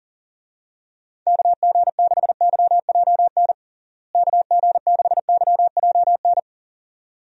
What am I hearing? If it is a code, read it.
Morse code, heard as a keyed tone, about 40 wpm, 700 Hz. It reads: KG6YJN KG6YJN
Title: USA Call Signs - 6 Characters - Repeated 2x - 40wpm